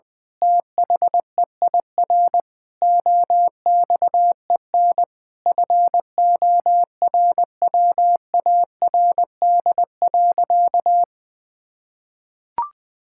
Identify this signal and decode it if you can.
Morse code, heard as a keyed tone, about 20 wpm, 700 Hz. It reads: THEIR OXEN FORWARD.
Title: Pangram Challenge v2 - 20wpm